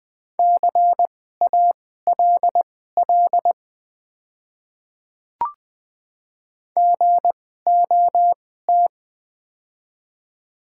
Morse code, heard as a keyed tone, about 20 wpm, spaced as for 15 wpm, 700 Hz. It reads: CALL GOT
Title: 2x Character Spacing - Top 300 Words - 20wpm